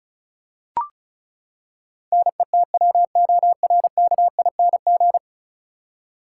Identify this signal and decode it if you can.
Morse code, heard as a keyed tone, about 34 wpm, 700 Hz. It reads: NETWORKING
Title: Practice QSO - Exchange 3 - More Info - 35wpm